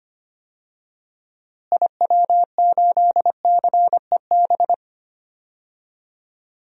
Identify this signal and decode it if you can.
Morse code, heard as a keyed tone, about 25 wpm, 700 Hz. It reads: IW8CEB
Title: DXCC-100 Call Signs - 6 Characters - Repeated 2x - 25wpm